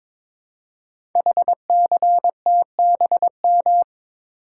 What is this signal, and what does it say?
Morse code, heard as a keyed tone, about 22 wpm, 700 Hz. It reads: HCTBM
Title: Code Groups - 5 Letters - 22wpm